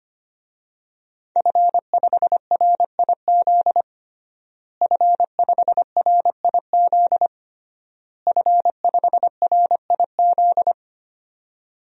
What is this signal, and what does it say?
Morse code, heard as a keyed tone, about 25 wpm, 700 Hz. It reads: F5RIZ F5RIZ F5RIZ
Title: DXCC-100 Call Signs - 5 Characters - Repeated 3x - 25wpm